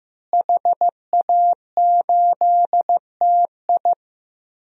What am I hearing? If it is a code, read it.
Morse code, heard as a keyed tone, about 15 wpm, 700 Hz. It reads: HA8TI
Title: DXCC-100 Call Signs - 5 Characters - Repeated 3x - 15wpm